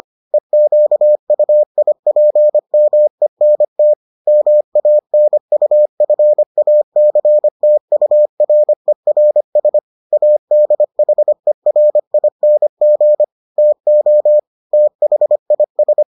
Morse code, heard as a keyed tone, about 25 wpm, 600 Hz. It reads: EQUIPMENT MANUFACTURERS ADHERING TO THIS